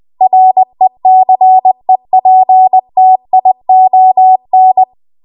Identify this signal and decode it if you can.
Morse code, heard as a keyed tone, about 20 wpm, 750 Hz. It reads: RECEPTION
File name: Archive / 20 WPM / 20WPM